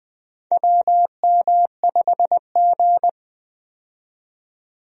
Morse code, heard as a keyed tone, about 20 wpm, 700 Hz. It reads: WM5G
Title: USA Call Signs - 4 Characters - 20wpm